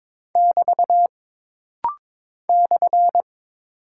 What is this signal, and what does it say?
Morse code, heard as a keyed tone, about 22 wpm, 700 Hz. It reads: <BT> /
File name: practice / 22wpm / Single Punctuation-Prosign - Rapid-Fire - 22wpm